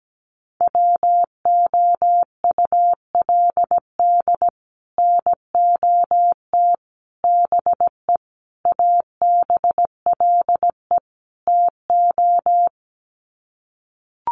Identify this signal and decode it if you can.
Morse code, heard as a keyed tone, about 17 wpm, 700 Hz. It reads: WOULD NOT BE ABLE TO
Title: Sets of 5 Words 17wpm